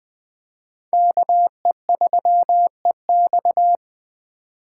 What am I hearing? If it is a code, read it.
Morse code, heard as a keyed tone, about 20 wpm, 700 Hz. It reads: KE3EX
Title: Speed Racing - USA Call Signs 17wpm